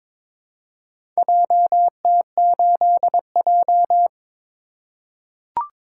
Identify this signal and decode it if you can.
Morse code, heard as a keyed tone, about 22 wpm, 700 Hz. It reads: JT8J
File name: practice / 22wpm / Four Letter-Number Combinations - 22wpm